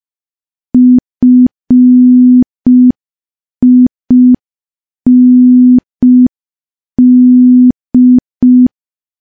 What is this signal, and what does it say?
Morse code, heard as a keyed tone, about 5 wpm, 250 Hz. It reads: FIND
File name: morse / morse-training-05WPM